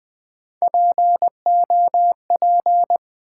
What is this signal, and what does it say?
Morse code, heard as a keyed tone, about 20 wpm, 700 Hz. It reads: POP
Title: Build A Word - 1 Letter At A Time - 3-Letter Words - 20wpm